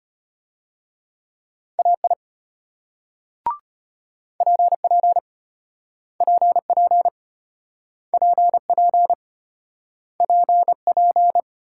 Morse code, heard as a keyed tone, about 33 wpm, 700 Hz. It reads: AI PP PP PP PP